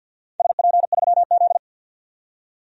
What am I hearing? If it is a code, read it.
Morse code, heard as a keyed tone, about 52 wpm, 700 Hz. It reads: SP3Z
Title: DXCC-100 Call Signs - 4 Characters - 50wpm